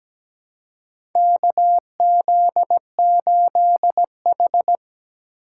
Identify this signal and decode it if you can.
Morse code, heard as a keyed tone, about 17 wpm, 700 Hz. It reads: KZ8H
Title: USA Call Signs - 2x Character Spacing - 17wpm